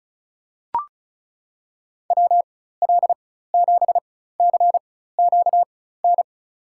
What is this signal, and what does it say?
Morse code, heard as a keyed tone, about 36 wpm, spaced as for 16 wpm, 700 Hz. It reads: WL7CQN
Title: USA Call Signs - 4x Character Spacing - 35wpm